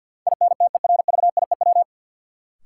Morse code, heard as a keyed tone, about 52 wpm, 700 Hz. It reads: INTERVIEW
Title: Sentences from Top 1600 Words - New Word Drill - 50wpm